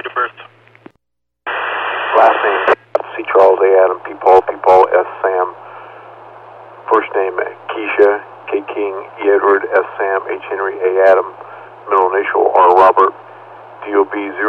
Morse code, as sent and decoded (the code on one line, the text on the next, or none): none